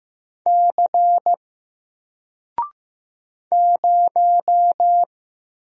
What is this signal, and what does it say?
Morse code, heard as a keyed tone, about 15 wpm, 700 Hz. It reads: C 0
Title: Single Character (Letter:Number:Punctuation:Prosign) 15wpm